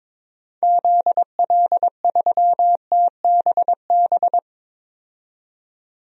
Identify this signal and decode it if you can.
Morse code, heard as a keyed tone, about 23 wpm, 700 Hz. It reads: ZL3TBB
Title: DXCC-100 Call Signs - 6 Characters - 22wpm